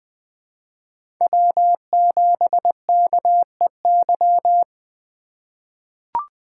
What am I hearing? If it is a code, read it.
Morse code, heard as a keyed tone, about 20 wpm, 700 Hz. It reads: W7KEY
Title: USA Call Signs - 2x Character Spacing - 20wpm